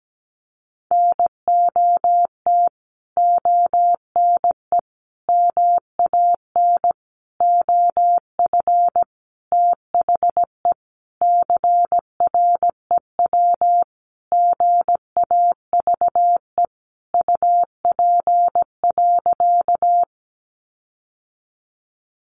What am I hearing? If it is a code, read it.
Morse code, heard as a keyed tone, about 17 wpm, 700 Hz. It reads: NOT ONE MAN OF THE CREW GAVE UP.
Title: Sentences from Top 1500 Words 17wpm